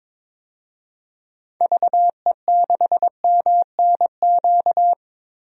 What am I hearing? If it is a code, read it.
Morse code, heard as a keyed tone, about 22 wpm, 700 Hz. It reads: VE6MNQ
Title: DXCC-100 Call Signs - 6 Characters - 22wpm